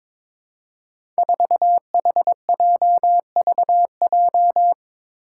Morse code, heard as a keyed tone, about 22 wpm, 700 Hz. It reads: 4HJVJ
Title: Five Challenging Letter-Number Combinations 22wpm